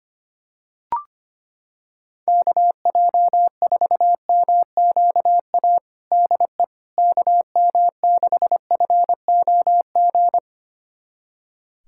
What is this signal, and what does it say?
Morse code, heard as a keyed tone, about 25 wpm, 700 Hz. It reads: KJ4MQA DE KM6FOG
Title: USA Callsign DE Callsign - 6 Characters - 25wpm